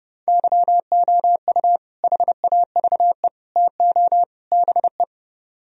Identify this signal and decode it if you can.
Morse code, heard as a keyed tone, about 29 wpm, 700 Hz. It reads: YOU HAVE TO BE